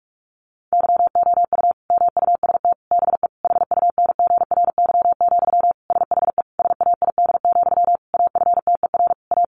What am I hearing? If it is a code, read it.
Morse code, heard as a keyed tone, about 44 wpm, 700 Hz. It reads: YOU MUST BE HUNGRY, SHE SAID, AFTER A
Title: Sentences from Top 1400 Words 45wpm